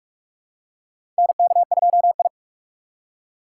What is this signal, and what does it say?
Morse code, heard as a keyed tone, about 46 wpm, 700 Hz. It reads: NK2I
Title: USA Call Signs - 4 Characters - 3x Character Spacing - 45wpm